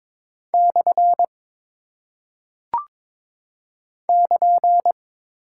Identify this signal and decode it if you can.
Morse code, heard as a keyed tone, about 22 wpm, 700 Hz. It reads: / <KN>
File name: practice / 22wpm / Single Punctuation-Prosign - 22wpm